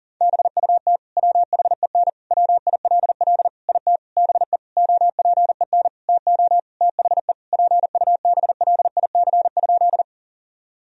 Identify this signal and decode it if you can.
Morse code, heard as a keyed tone, about 40 wpm, 700 Hz. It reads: BUT WHEN WILL IT BE OPEN TO THE PUBLIC?
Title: Sentences from Top 1000 Words - Review Part 3 - 40wpm